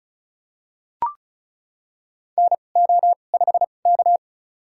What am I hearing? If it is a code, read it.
Morse code, heard as a keyed tone, about 36 wpm, spaced as for 25 wpm, 700 Hz. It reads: NO5K
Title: USA Call Signs - 4 Characters - 2x Character Spacing - 35wpm